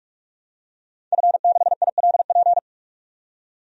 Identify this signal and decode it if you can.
Morse code, heard as a keyed tone, about 45 wpm, 700 Hz. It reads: F6ILP
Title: DXCC-100 Call Signs - 5 Characters - 45wpm